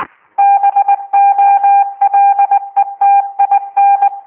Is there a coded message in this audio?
Yes: Morse code from 0.4 to 4.1 s, heard as a keyed tone, about 20 wpm, 800 Hz. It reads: BOLETIN